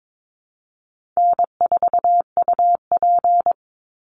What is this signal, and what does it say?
Morse code, heard as a keyed tone, about 22 wpm, 700 Hz. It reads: N4UP
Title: Easy Call Signs - 22wpm